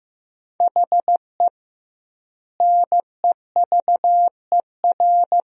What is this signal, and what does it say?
Morse code, heard as a keyed tone, about 15 wpm, 700 Hz. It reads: HE NEVER
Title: 2x Word-Spacing - Sentences from Top 400 Words - 15wpm